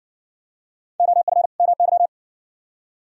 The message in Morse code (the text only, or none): OF MY